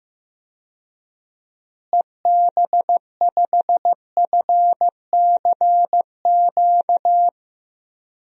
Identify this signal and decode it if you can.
Morse code, heard as a keyed tone, about 15 wpm, 700 Hz. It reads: EB5FCQ
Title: DXCC-100 Call Signs - 6 Characters - Repeated 2x - 15wpm